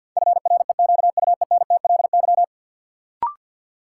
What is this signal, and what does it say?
Morse code, heard as a keyed tone, about 53 wpm, 700 Hz. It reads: FREQUENTLY
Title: Sentences from Top 1300 Words - New Word Drill Alt - 50wpm